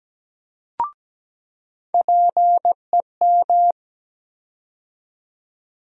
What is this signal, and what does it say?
Morse code, heard as a keyed tone, about 17 wpm, 700 Hz. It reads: PEM
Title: USA Call Sign Postfixes - 3 Letters 17wpm